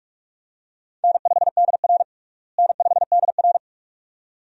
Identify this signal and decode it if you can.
Morse code, heard as a keyed tone, about 42 wpm, 700 Hz. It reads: N5DR N5DR
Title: USA Call Signs - 4 Characters - Repeated 2x - 45wpm